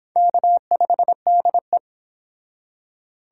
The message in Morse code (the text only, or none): K5DE